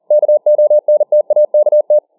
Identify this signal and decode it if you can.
Morse code, heard as a keyed tone, about 40 wpm, 600 Hz. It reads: KONTAKT